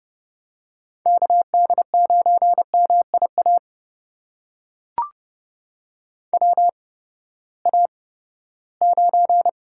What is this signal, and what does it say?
Morse code, heard as a keyed tone, about 30 wpm, spaced as for 17 wpm, 700 Hz. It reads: KD9MIA WA9
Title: USA Call Signs - 6 Characters - 8x Character Spacing - 30wpm